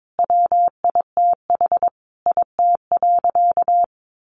Morse code, heard as a keyed tone, about 22 wpm, 700 Hz. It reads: WITH IT.
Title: Sentences from Top 1400 Words 22wpm